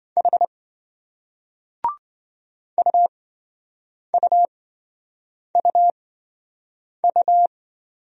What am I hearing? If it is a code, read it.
Morse code, heard as a keyed tone, about 28 wpm, spaced as for 7 wpm, 700 Hz. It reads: H UUUU